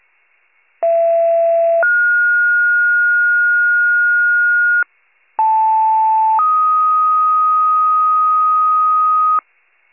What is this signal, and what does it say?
Two-tone page, heard as a one-second tone, then a three-second tone, about 670 then 1430 Hz, about 880 then 1250 Hz, with noise. It is heard from 0.8 to 9.4 s.
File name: pager / 76 FIRE / wcalert